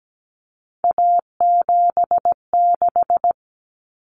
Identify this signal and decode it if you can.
Morse code, heard as a keyed tone, about 17 wpm, 700 Hz. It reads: A76
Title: Common Cut Numbers - 3-Digit 17wpm